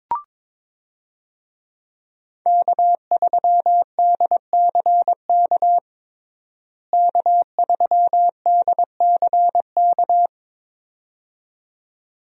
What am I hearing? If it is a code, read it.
Morse code, heard as a keyed tone, about 21 wpm, 700 Hz. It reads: K3DCK K3DCK
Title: USA Call Signs - 5 Characters - Repeated 2x - 22wpm